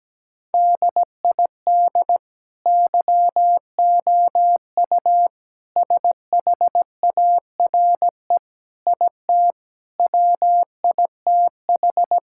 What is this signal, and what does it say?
Morse code, heard as a keyed tone, about 17 wpm, 700 Hz. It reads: DID YOU SHARE IT WITH
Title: Sentences from Top 1000 Words - Review Part 6 - 17wpm